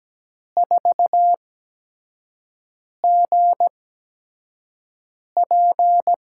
Morse code, heard as a keyed tone, about 17 wpm, spaced as for 5 wpm, 700 Hz. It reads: 4GP
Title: USA Call Signs - 5 Characters - 8x Character Spacing - 17wpm